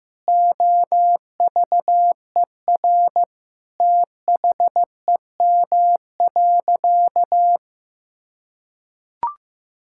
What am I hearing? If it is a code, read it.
Morse code, heard as a keyed tone, about 15 wpm, 700 Hz. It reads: OVER THEM.
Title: Sentences from Top 1000 Words - Review Part 6 - 15wpm